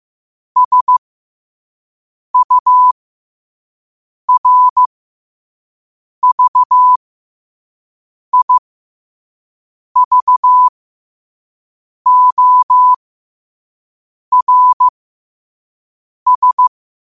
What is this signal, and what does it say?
Morse code, heard as a keyed tone, about 15 wpm, spaced as for 5 wpm, 1000 Hz. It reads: SURVIVORS
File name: qotdfiles / 5wpm